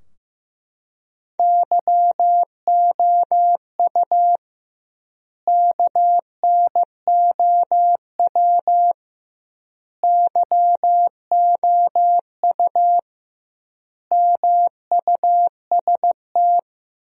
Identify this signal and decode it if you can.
Morse code, heard as a keyed tone, about 15 wpm, 700 Hz. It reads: YOU KNOW YOU MUST
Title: 2x Word-Spacing - Sentences from Top 500 Words - 15wpm